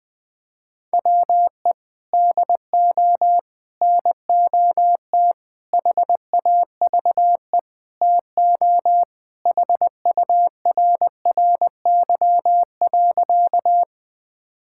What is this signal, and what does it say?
Morse code, heard as a keyed tone, about 20 wpm, 700 Hz. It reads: WE DO NOT HAVE TO HURRY.
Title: Sentences with 5-Word Phrases - Spoken Beginning - 20wpm